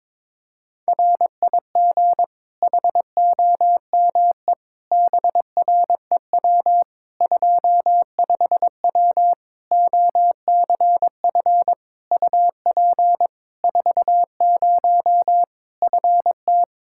Morse code, heard as a keyed tone, about 22 wpm, 700 Hz. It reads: RIG HOME BREW 25W OCF UP 40 FT
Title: Practice QSO - Exchange 2 - Info - 22wpm